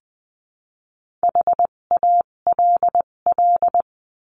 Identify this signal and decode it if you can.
Morse code, heard as a keyed tone, about 20 wpm, spaced as for 17 wpm, 700 Hz. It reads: HALL
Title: 257 Lesson 42b - Words 14-20wpm